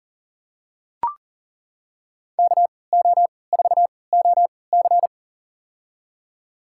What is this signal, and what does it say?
Morse code, heard as a keyed tone, about 40 wpm, spaced as for 23 wpm, 700 Hz. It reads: KO4OC
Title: USA Call Signs - 3x Character Spacing - 40wpm